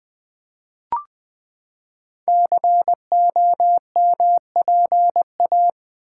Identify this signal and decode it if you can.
Morse code, heard as a keyed tone, about 20 wpm, 700 Hz. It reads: COMPA
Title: Build A Word - 5 Letters At A Time - 10-Letter Words - 20wpm